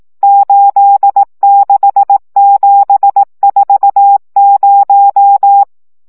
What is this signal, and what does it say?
Morse code, heard as a keyed tone, about 18 wpm, 800 Hz. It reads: 86740
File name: morze vizsga